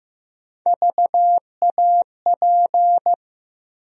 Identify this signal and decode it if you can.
Morse code, heard as a keyed tone, about 15 wpm, 700 Hz. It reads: VAP
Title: USA Call Sign Postfixes - 3 Letters 15wpm